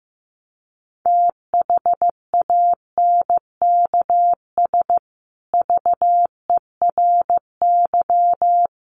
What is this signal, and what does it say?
Morse code, heard as a keyed tone, about 15 wpm, 700 Hz. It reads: THANKS VERY
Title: Sets of 5 Words Encore2 15wpm